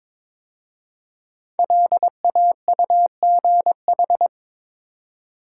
Morse code, heard as a keyed tone, about 22 wpm, 700 Hz. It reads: LAUGH
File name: 5 Letter Words - First 3 Letters Given - 22wpm